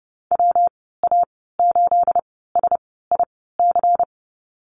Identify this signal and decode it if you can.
Morse code, heard as a keyed tone, about 30 wpm, spaced as for 17 wpm, 700 Hz. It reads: WA8SIC
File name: Lesson 37g - Letter x - Call Signs 30wpm